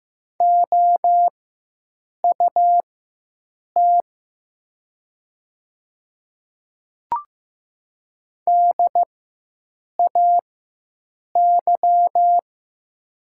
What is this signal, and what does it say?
Morse code, heard as a keyed tone, about 15 wpm, spaced as for 7 wpm, 700 Hz. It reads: OUT DAY
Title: 4x Character Spacing - Top 100 Words - 15wpm